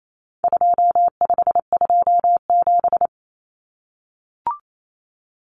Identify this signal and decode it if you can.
Morse code, heard as a keyed tone, about 28 wpm, 700 Hz. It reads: 2527
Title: Numbers - 4-Digit 28wpm